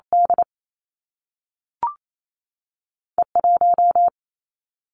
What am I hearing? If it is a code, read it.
Morse code, heard as a keyed tone, about 28 wpm, 700 Hz. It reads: D E 1
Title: Two Challenging Letter-Number Combinations 28wpm